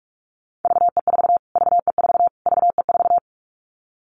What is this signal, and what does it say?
Morse code, heard as a keyed tone, about 44 wpm, 700 Hz. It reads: VE4 VE4 VE4
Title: Callsign Prefixes for Top 25 DXCC - Repeated 3x - 45wpm